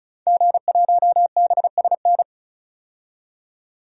Morse code, heard as a keyed tone, about 34 wpm, 700 Hz. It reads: G1BSN
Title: DXCC-100 Call Signs - 5 Characters - 35wpm